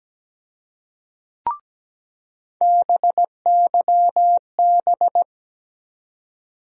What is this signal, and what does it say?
Morse code, heard as a keyed tone, about 17 wpm, 700 Hz. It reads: BYB
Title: USA Call Sign Postfixes - 3 Letters 17wpm